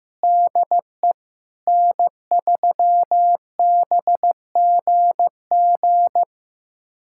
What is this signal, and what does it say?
Morse code, heard as a keyed tone, about 15 wpm, 700 Hz. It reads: DE N3BGG
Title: USA Callsign DE Callsign 15wpm